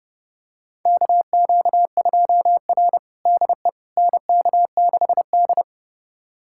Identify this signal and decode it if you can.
Morse code, heard as a keyed tone, about 31 wpm, 700 Hz. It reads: KQ2R DE NK6D
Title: USA Callsign DE Callsign - 4 Characters - 30wpm